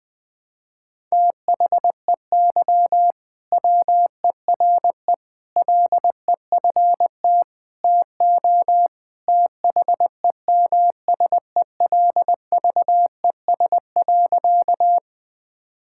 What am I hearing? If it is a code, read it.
Morse code, heard as a keyed tone, about 20 wpm, 700 Hz. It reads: THEY WERE LEFT TO THEMSELVES.
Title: Sentences from Top 1000 Words - Review Part 5 - 20wpm